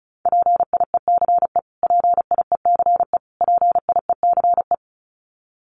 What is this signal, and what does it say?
Morse code, heard as a keyed tone, about 34 wpm, 700 Hz. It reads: PIECE PIECE PIECE
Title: Top 500 Words Repeated 35wpm